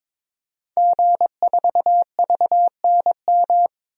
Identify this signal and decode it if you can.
Morse code, heard as a keyed tone, about 21 wpm, 700 Hz. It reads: G4VNM
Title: DXCC-100 Call Signs - 5 Characters - 22wpm